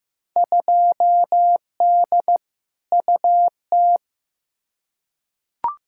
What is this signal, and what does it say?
Morse code, heard as a keyed tone, about 15 wpm, 700 Hz. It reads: 2D UT
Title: Field Day Exchanges Encore - 15wpm